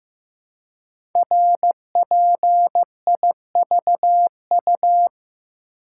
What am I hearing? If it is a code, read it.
Morse code, heard as a keyed tone, about 15 wpm, 700 Hz. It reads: RPIVU
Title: Code Groups - 5 Letters - 15wpm